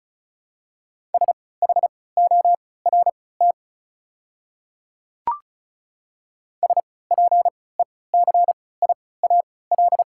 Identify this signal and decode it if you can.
Morse code, heard as a keyed tone, about 34 wpm, spaced as for 20 wpm, 700 Hz. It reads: SHORT SPECIAL
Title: 3x Character Spacing - Top 500 Words - 35wpm